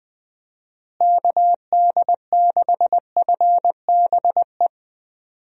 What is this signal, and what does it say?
Morse code, heard as a keyed tone, about 20 wpm, 700 Hz. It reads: KD6FBE
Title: USA Call Signs - 6 Characters - Repeated 2x - 20wpm